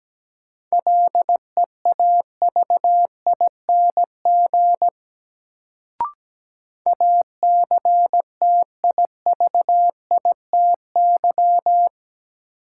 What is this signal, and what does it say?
Morse code, heard as a keyed tone, about 17 wpm, 700 Hz. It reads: LEAVING ACTIVITY
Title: Sentences from Top 1000 Words - Review Part 4 - New Word Drill - 17wpm